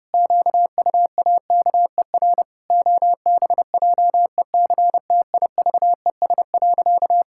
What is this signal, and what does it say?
Morse code, heard as a keyed tone, about 30 wpm, 700 Hz. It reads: QUAKER OBJECTIVES.
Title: Pangram Challenge v2 - 30wpm